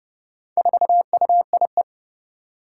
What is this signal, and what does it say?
Morse code, heard as a keyed tone, about 30 wpm, 700 Hz. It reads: 4UIE
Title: Four Challenging Letter-Number Combinations 30wpm